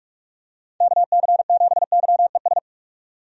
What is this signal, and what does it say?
Morse code, heard as a keyed tone, about 46 wpm, 700 Hz. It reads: KC7YES
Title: USA Call Signs - 6 Characters - 45wpm